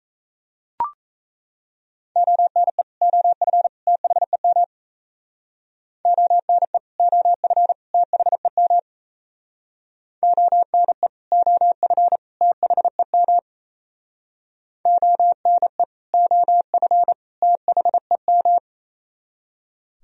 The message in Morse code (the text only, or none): ONE OF THEM ONE OF THEM ONE OF THEM ONE OF THEM